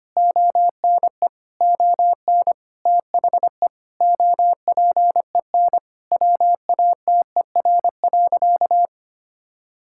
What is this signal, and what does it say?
Morse code, heard as a keyed tone, about 25 wpm, 700 Hz. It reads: ONE ON THE OPEN WATER.
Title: Sentences from Top 1000 Words - Review Part 6 - 25wpm